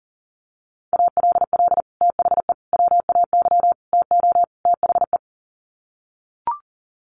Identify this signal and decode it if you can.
Morse code, heard as a keyed tone, about 40 wpm, 700 Hz. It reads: ALL THE WAY TO THE
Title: Sets of 5 Words 40wpm